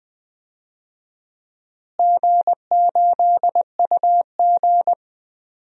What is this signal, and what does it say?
Morse code, heard as a keyed tone, about 20 wpm, 700 Hz. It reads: G8UG
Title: DXCC-100 Call Signs - 4 Characters - Repeated 2x - 20wpm